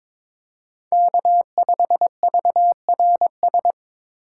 Morse code, heard as a keyed tone, about 21 wpm, 700 Hz. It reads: K5VRS
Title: USA Call Signs - 6x Character Spacing - 22wpm